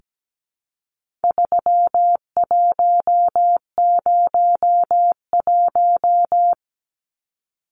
Morse code, heard as a keyed tone, about 17 wpm, 700 Hz. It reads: 3101
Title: Numbers - 4-Digit 17wpm